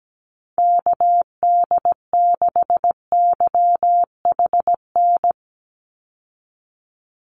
Morse code, heard as a keyed tone, about 17 wpm, 700 Hz. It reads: KD6YHN
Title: USA Call Signs 17wpm